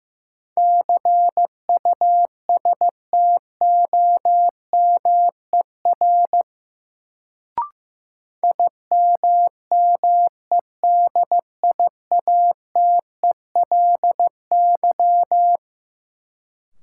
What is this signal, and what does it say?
Morse code, heard as a keyed tone, about 15 wpm, 700 Hz. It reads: CUSTOMER IMMEDIATELY
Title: Sentences from Top 1100 Words - New Word Drill Alt - 15wpm